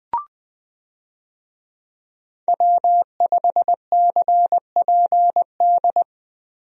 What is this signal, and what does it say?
Morse code, heard as a keyed tone, about 21 wpm, 700 Hz. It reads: W5CPD
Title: USA Call Signs - 5 Characters - Repeated 3x - 20wpm